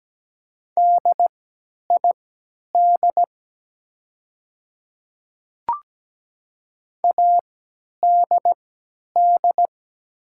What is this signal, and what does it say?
Morse code, heard as a keyed tone, about 17 wpm, spaced as for 10 wpm, 700 Hz. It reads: DID ADD
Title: 3x Character Spacing - Top 200 Words - 17wpm